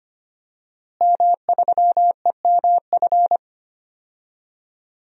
Morse code, heard as a keyed tone, about 25 wpm, 700 Hz. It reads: M3EMF
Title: DXCC-100 Call Signs - 5 Characters - Repeated 3x - 25wpm